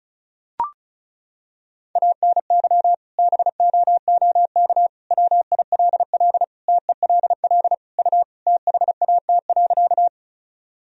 Morse code, heard as a keyed tone, about 36 wpm, 700 Hz. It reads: ANY BOOK WILL TELL U THAT.
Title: Sentences from Top 300 Words and Abbreviations 35wpm